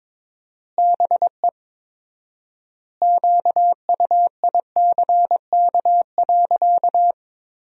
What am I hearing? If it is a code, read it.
Morse code, heard as a keyed tone, about 22 wpm, 700 Hz. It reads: BE QUICK.
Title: 4x Word-Spacing - Sentences from Top 500 Words - 22wpm